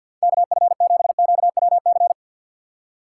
Morse code, heard as a keyed tone, about 47 wpm, 700 Hz. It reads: KF7QWC
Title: USA Call Signs - 6 Characters - 3x Character Spacing - 50wpm